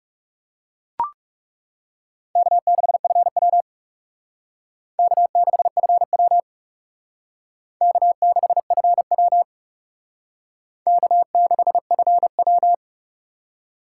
Morse code, heard as a keyed tone, about 37 wpm, 700 Hz. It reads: K6FW K6FW K6FW K6FW
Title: Speed Racing - USA 1x2 Call Signs 30wpm